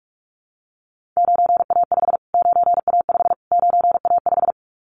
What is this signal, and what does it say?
Morse code, heard as a keyed tone, about 44 wpm, 700 Hz. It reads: 9A5 9A5 9A5
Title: Callsign Prefixes for Top 25 DXCC - Repeated 3x - 45wpm